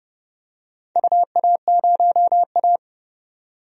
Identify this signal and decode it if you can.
Morse code, heard as a keyed tone, about 30 wpm, 700 Hz. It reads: UA0A
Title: DXCC-100 Call Signs - 4 Characters - 30wpm